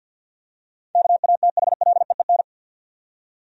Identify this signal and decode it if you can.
Morse code, heard as a keyed tone, about 51 wpm, 700 Hz. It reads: KATHLEEN
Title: USA Common Names - 50wpm